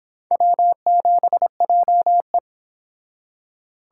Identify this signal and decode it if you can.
Morse code, heard as a keyed tone, about 28 wpm, 700 Hz. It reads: W7JE